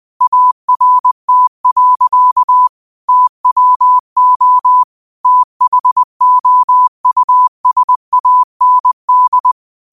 Morse code, heard as a keyed tone, about 20 wpm, 1000 Hz. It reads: ART. TWO THOUSAND
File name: qotdfiles / 20wpm